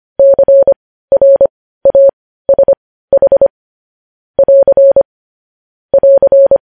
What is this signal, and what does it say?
Morse code, heard as a keyed tone, about 25 wpm, spaced as for 15 wpm, 550 Hz. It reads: CRASH <AR> <AR>
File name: podcast / News headlines-1x-25x15